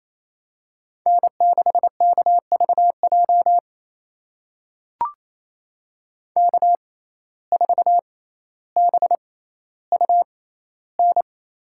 Morse code, heard as a keyed tone, about 28 wpm, spaced as for 14 wpm, 700 Hz. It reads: N6KVJ K4BUN
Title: USA Call Signs - 5 Characters - 6x Character Spacing - 28wpm